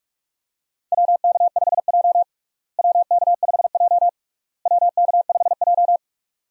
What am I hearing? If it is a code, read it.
Morse code, heard as a keyed tone, about 43 wpm, 700 Hz. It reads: WK5J WK5J WK5J
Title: USA Call Signs - 4 Characters - Repeated 3x - 45wpm